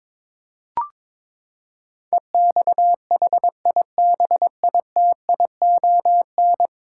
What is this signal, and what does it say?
Morse code, heard as a keyed tone, about 22 wpm, 700 Hz. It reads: EXHIBITION
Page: morse-code-ninja-practice-22wpm